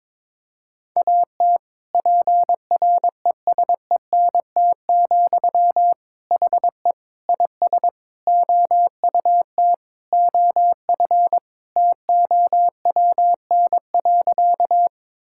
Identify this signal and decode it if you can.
Morse code, heard as a keyed tone, about 21 wpm, 700 Hz. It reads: AT PRESENT, HE IS OUT OF TOWN.
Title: Sentences from Top 1000 Words - Review Part 5 - 22wpm